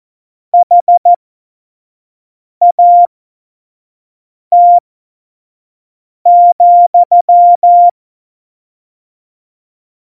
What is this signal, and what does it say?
Morse code, heard as a keyed tone, about 14 wpm, spaced as for 5 wpm, 700 Hz. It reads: HAT,
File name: Morsekurs / Witze / Witz 19 Mai 2020, 25 F70